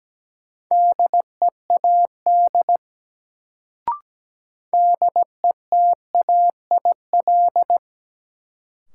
Morse code, heard as a keyed tone, about 17 wpm, 700 Hz. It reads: DEAD DETAIL
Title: Sentences from Top 1500 Words - New Word Drill - 17wpm